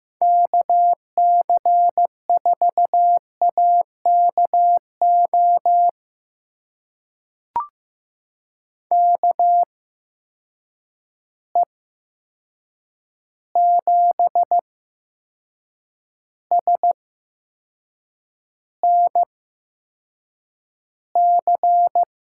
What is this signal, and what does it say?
Morse code, heard as a keyed tone, about 15 wpm, spaced as for 6 wpm, 700 Hz. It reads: KC4AKO KE7SNC
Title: USA Call Signs - 6 Characters - 8x Character Spacing - 15wpm